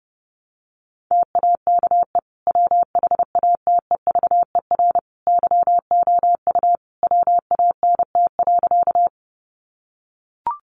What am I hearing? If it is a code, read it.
Morse code, heard as a keyed tone, about 30 wpm, 700 Hz. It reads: TAKE WHATEVER YOU WANT.
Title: Sentences from Top 1100 Words 30wpm